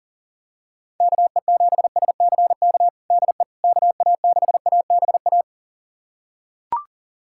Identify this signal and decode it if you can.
Morse code, heard as a keyed tone, about 40 wpm, 700 Hz. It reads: KE7SCK DE KA6ABA
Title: USA Callsign DE Callsign - 6 Characters - 40wpm